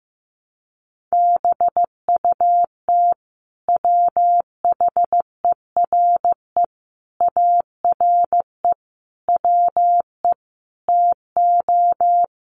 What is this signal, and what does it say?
Morse code, heard as a keyed tone, about 15 wpm, 700 Hz. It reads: BUT WHERE ARE WE TO
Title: Sentences from Top 200 Words 15wpm